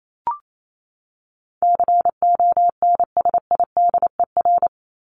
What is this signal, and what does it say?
Morse code, heard as a keyed tone, about 28 wpm, 700 Hz. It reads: CONSIDER
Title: ING Suffix - Top 100 Words with Base Words 28wpm